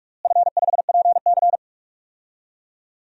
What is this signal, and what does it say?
Morse code, heard as a keyed tone, about 46 wpm, 700 Hz. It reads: F5PC
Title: DXCC-100 Call Signs - 4 Characters - Repeated 3x - 45wpm